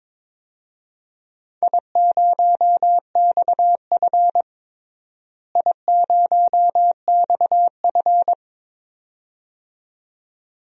Morse code, heard as a keyed tone, about 22 wpm, 700 Hz. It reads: I0XF I0XF
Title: DXCC-100 Call Signs - 4 Characters - Repeated 2x - 22wpm